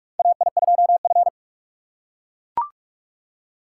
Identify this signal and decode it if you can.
Morse code, heard as a keyed tone, about 42 wpm, 700 Hz. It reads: AI2F
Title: USA Call Signs - 4 Characters - 4x Character Spacing - 45wpm